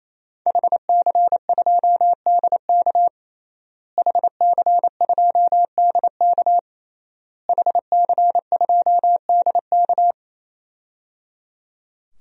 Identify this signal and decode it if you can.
Morse code, heard as a keyed tone, about 28 wpm, 700 Hz. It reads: HC2DK HC2DK HC2DK
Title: DXCC-100 Call Signs - 5 Characters - Repeated 3x - 28wpm